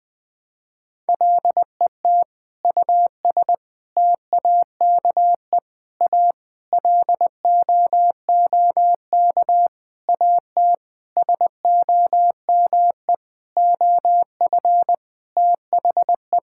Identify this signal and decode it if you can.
Morse code, heard as a keyed tone, about 21 wpm, 700 Hz. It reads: LET US TAKE A LOOK AT SOME OF THE
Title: Sentences with 8-Word Phrases - Spoken Beginning - 20wpm